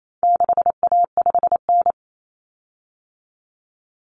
Morse code, heard as a keyed tone, about 28 wpm, 700 Hz. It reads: 6A5N
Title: Common Cut Numbers - 4-Digit 28wpm